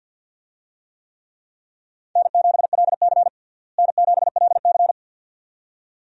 Morse code, heard as a keyed tone, about 52 wpm, 700 Hz. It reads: N7LC N7LC
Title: USA Call Signs - 4 Characters - Repeated 2x - 50wpm